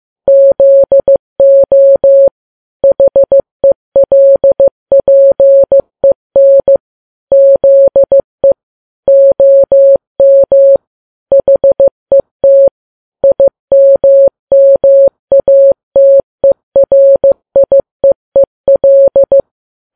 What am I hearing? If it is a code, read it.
Morse code, heard as a keyed tone, about 15 wpm, 550 Hz. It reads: ZO HELPEN ZE OM HET IMMATERIEEL
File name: Files Web tls / Morse on9fv / morse15kps